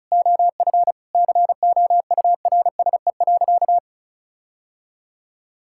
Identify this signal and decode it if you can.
Morse code, heard as a keyed tone, about 34 wpm, 700 Hz. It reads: OF COURSE.
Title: Sentences from Top 1600 Words 35wpm